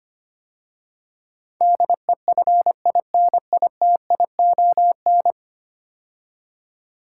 Morse code, heard as a keyed tone, about 25 wpm, 700 Hz. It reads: DEFINITION
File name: 10 Letter Words - First 3 Letters Given - 25wpm